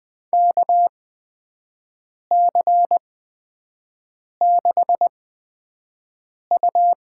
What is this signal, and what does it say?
Morse code, heard as a keyed tone, about 21 wpm, spaced as for 5 wpm, 700 Hz. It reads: KC6U
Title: USA Call Signs - 8x Character Spacing - 20wpm